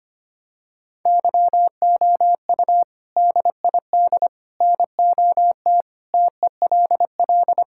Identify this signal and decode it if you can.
Morse code, heard as a keyed tone, about 25 wpm, 700 Hz. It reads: YOU DID NOT TELL
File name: practice / 25wpm / Sentences with 3-Word Phrases - 25wpm